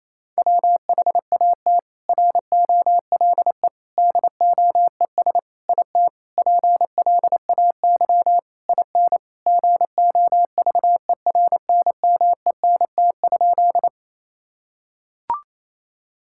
Morse code, heard as a keyed tone, about 28 wpm, 700 Hz. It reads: WHAT ROLE DOES IT PLAY IN GOVERNMENT?
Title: Sentences from Top 1000 Words - Review Part 6 - 28wpm